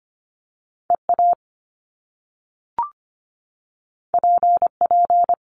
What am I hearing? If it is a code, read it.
Morse code, heard as a keyed tone, about 25 wpm, 700 Hz. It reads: EA PP
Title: Top Two Letter Combinations in Words 25wpm